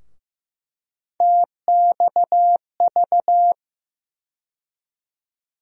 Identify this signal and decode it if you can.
Morse code, heard as a keyed tone, about 15 wpm, 700 Hz. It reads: TXV
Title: USA Call Sign Postfixes - 3 Letters 15wpm